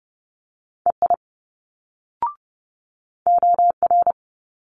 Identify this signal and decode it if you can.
Morse code, heard as a keyed tone, about 30 wpm, 700 Hz. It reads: EI OR
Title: Multi-letter Phonemes Quick Fire 30wpm